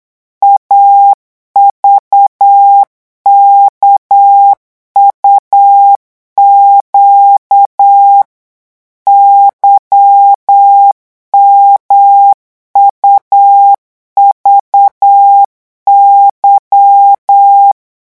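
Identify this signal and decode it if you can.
Morse code, heard as a keyed tone, about 8 wpm, 800 Hz. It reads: AVKUQ YMUVY
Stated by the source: Morse code revision lesson at 8 words per minute, from a French amateur radio course.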